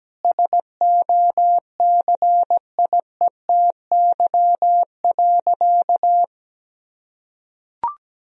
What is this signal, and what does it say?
Morse code, heard as a keyed tone, about 17 wpm, 700 Hz. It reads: SOCIETY.
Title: Sentences from Top 1000 Words - Review Part 4 - 17wpm